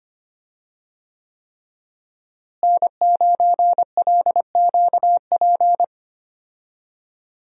Morse code, heard as a keyed tone, about 25 wpm, 700 Hz. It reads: N9LQP
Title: USA Call Signs - 5 Characters - Repeated 3x - 25wpm